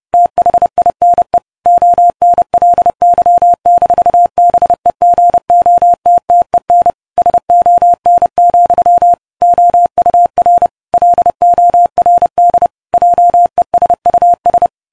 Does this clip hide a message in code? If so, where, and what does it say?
Yes: Morse code from 0.1 to 14.7 s, heard as a keyed tone, about 30 wpm, 700 Hz. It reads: THINE ONLY-BEGOTTEN SON, OUR LORD JESUS